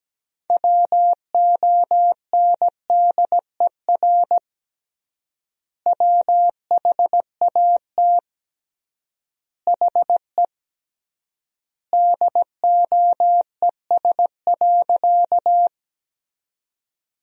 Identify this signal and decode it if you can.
Morse code, heard as a keyed tone, about 17 wpm, 700 Hz. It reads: WONDER WHAT HE DOES.
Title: 3x Word-Spacing - Sentences from Top 500 Words - 17wpm